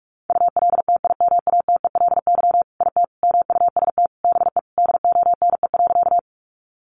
Morse code, heard as a keyed tone, about 44 wpm, 700 Hz. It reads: ULTIMATELY IT MUST BE DONE.
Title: Sentences from Top 1200 Words 45wpm